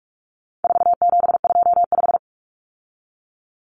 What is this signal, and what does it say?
Morse code, heard as a keyed tone, about 44 wpm, 700 Hz. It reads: 4725
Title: Numbers - 4-Digit 45wpm